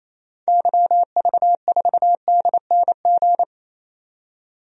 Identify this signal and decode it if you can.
Morse code, heard as a keyed tone, about 28 wpm, 700 Hz. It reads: YV4DNG
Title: DXCC-100 Call Signs - 6 Characters - 28wpm